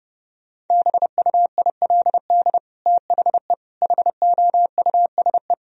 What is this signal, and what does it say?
Morse code, heard as a keyed tone, about 30 wpm, 700 Hz. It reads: BUILD THE HOUSE
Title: Sentences from Top 1000 Words - Review Part 1 - 30wpm